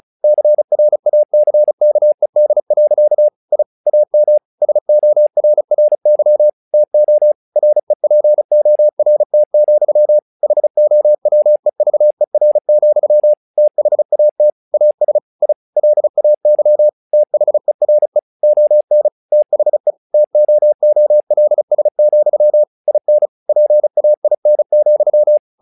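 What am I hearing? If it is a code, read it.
Morse code, heard as a keyed tone, about 35 wpm, 600 Hz. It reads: CRACKED. I AM SORRY TO REPORT, HOWEVER, THAT AS I LAY THERE ON THE TOOLS, IN PAIN,
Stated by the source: Morse practice text "Tower Trouble" sent at 35 WPM, from the SKCC - Straight Key Century Club learning center audio files.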